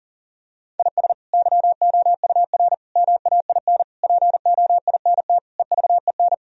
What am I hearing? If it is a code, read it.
Morse code, heard as a keyed tone, about 40 wpm, 700 Hz. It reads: IS YOUR MAIN POINT EVEN
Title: Sentences from Top 1000 Words - Review Part 2 - 40wpm